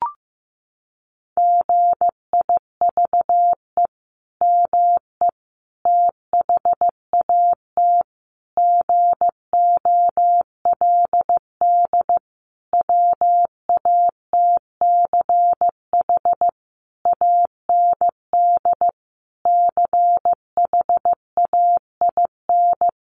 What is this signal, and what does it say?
Morse code, heard as a keyed tone, about 15 wpm, 700 Hz. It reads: GIVE ME THAT GOLD WATCH AND CHAIN
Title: Sentences from Top 1500 Words 15wpm